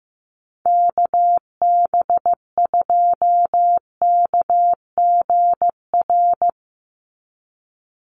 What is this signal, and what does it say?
Morse code, heard as a keyed tone, about 15 wpm, 700 Hz. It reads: KB2KGR
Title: USA Call Signs 15wpm